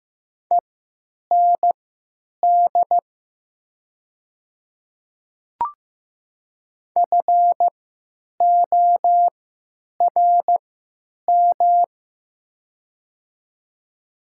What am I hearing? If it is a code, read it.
Morse code, heard as a keyed tone, about 15 wpm, spaced as for 9 wpm, 700 Hz. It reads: END FORM
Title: 3x Character Spacing - Top 500 Words - 15wpm